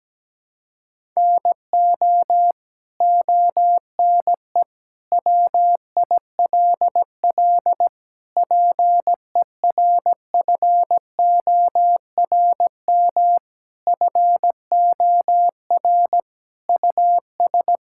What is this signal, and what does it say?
Morse code, heard as a keyed tone, about 17 wpm, 700 Hz. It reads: NO ONE WILL PERFORM FOR US
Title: Sentences from Top 1000 Words - Review Part 5 - 17wpm